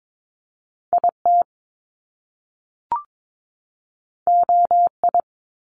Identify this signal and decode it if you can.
Morse code, heard as a keyed tone, about 22 wpm, 700 Hz. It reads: IT OI
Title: Top Two Letter Combinations in Words 22wpm